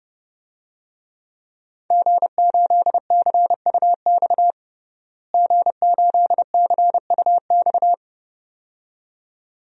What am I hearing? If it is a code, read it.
Morse code, heard as a keyed tone, about 30 wpm, 700 Hz. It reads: G8CUX G8CUX
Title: DXCC-100 Call Signs - 5 Characters - Repeated 2x - 30wpm